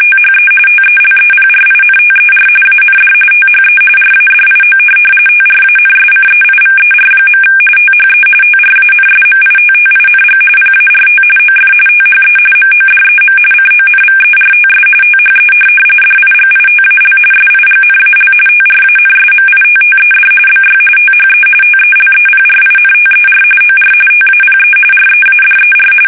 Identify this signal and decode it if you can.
RTTY, about 45.45 baud. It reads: THE TELEPRINTER AND THE RADIO TRANSCEIVER. THE TRANSMITTING PART OF THE MODEM CONVERTS THE DIGITAL SIGNAL TRANSMITTED BY THE TELEPRINTER OR TAPE READER
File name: fsk850 / teletype-RTTY